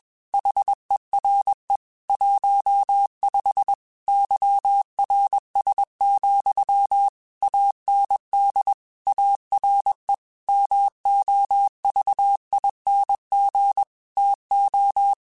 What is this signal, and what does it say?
Morse code, heard as a keyed tone, about 21 wpm, 800 Hz. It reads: HERE 15 YRS, AND ARE MOVING TO
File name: QSOs / QSOs-mp3s / WD0BGP 1@20